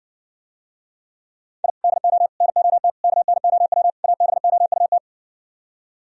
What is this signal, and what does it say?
Morse code, heard as a keyed tone, about 68 wpm, 700 Hz. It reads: I DO NOT KNOW ABOUT